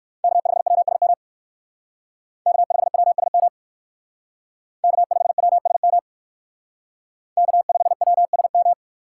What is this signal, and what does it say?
Morse code, heard as a keyed tone, about 60 wpm, 700 Hz. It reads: K5WSM K5WSM K5WSM K5WSM